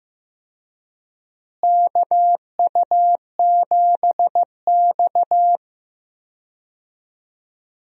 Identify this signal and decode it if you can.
Morse code, heard as a keyed tone, about 15 wpm, 700 Hz. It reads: KU7X
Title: USA Call Signs - 4 Characters - Repeated 2x - 15wpm